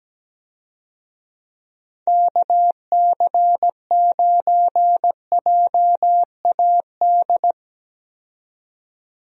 Morse code, heard as a keyed tone, about 17 wpm, 700 Hz. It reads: KC9JAD
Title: USA Call Signs - 6 Characters - 17wpm